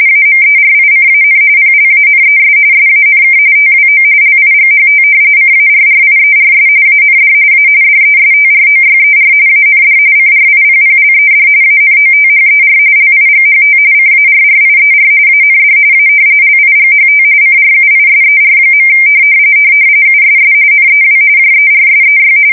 RTTY, about 45.45 baud. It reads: AMERICAN PLANES SANK TWO LIGHT CARRIERS AND A DESTROYER AND CRIPPLED ANOTHER LIGHT CARRIER AND A LIGHT CRUISER. USS NEW JERSEY
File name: fsk170 / battleshipnj